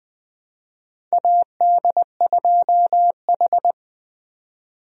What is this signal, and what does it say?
Morse code, heard as a keyed tone, about 20 wpm, 700 Hz. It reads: AD2H
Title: USA Call Signs - 4 Characters - 20wpm